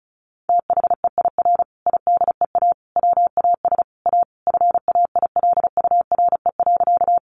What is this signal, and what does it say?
Morse code, heard as a keyed tone, about 34 wpm, 700 Hz. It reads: THEIR IDEA WAS A FAILURE.
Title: Sentences from Top 900 Words 35wpm